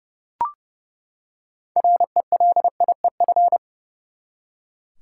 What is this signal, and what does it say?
Morse code, heard as a keyed tone, about 30 wpm, 700 Hz. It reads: RELIEF
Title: Six Letter Words - 30wpm